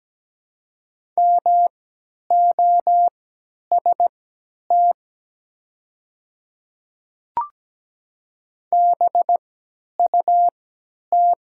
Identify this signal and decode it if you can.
Morse code, heard as a keyed tone, about 17 wpm, spaced as for 10 wpm, 700 Hz. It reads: MOST BUT
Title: 3x Character Spacing - Top 100 Words - 17wpm